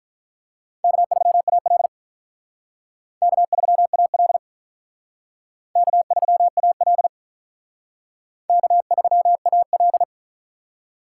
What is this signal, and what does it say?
Morse code, heard as a keyed tone, about 45 wpm, 700 Hz. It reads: K3AL K3AL K3AL K3AL